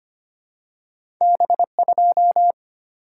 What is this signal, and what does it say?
Morse code, heard as a keyed tone, about 25 wpm, 690 Hz. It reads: B2
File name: Two Letter-Number Combinations - 25wpm